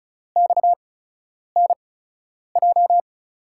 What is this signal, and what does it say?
Morse code, heard as a keyed tone, about 36 wpm, spaced as for 10 wpm, 700 Hz. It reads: XNJ
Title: USA Call Signs - 5 Characters - 8x Character Spacing - 35wpm